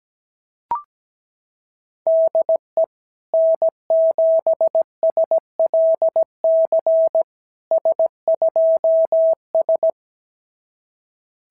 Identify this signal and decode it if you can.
Morse code, heard as a keyed tone, about 17 wpm, 690 Hz. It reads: DE N7SLC S2S
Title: SOTA QSO Practice 17wpm